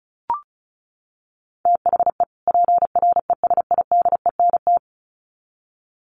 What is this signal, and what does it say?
Morse code, heard as a keyed tone, about 34 wpm, 700 Hz. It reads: THE PRESIDENT
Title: Sets of 2 Words 35wpm